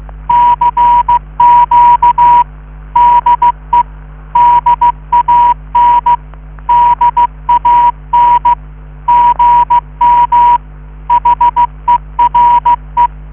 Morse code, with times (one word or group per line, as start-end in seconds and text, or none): 0.3-2.4 CQ
3.0-3.8 DE
4.4-6.2 DAN
6.7-8.5 DAN
9.1-10.6 GM
11.1-13.1 HERE